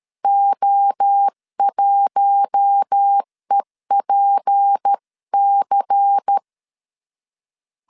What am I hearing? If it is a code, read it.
Morse code, heard as a keyed tone, about 13 wpm, 800 Hz. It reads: O1EPC
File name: CW / DET / 1 6 1500C S C 12wpm 3533